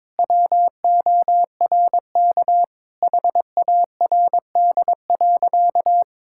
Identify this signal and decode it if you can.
Morse code, heard as a keyed tone, about 22 wpm, 700 Hz. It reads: WORK HARD.
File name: practice / 22wpm / Sentences with 3-Word Phrases - 22wpm